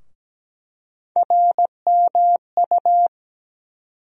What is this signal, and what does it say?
Morse code, heard as a keyed tone, about 17 wpm, 700 Hz. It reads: RMU
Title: USA Call Sign Postfixes - 3 Letters 17wpm